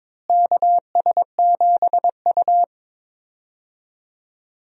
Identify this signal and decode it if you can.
Morse code, heard as a keyed tone, about 22 wpm, 700 Hz. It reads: KS7U
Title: USA Call Signs - 4 Characters - Repeated 3x - 22wpm